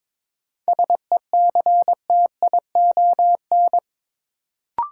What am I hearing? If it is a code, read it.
Morse code, heard as a keyed tone, about 23 wpm, 700 Hz. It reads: SECTION
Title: Sentences from Top 1000 Words - Review Part 5 - New Word Drill - 22wpm